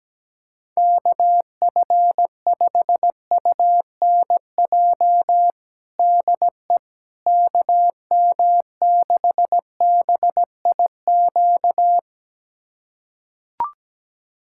Morse code, heard as a keyed tone, about 17 wpm, 700 Hz. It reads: KF5UNJ DE KM6BIQ
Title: USA Callsign DE Callsign 17wpm